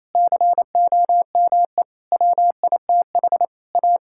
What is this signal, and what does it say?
Morse code, heard as a keyed tone, about 28 wpm, 700 Hz. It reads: COME WITH A
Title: Sentences from Top 1600 Words 28wpm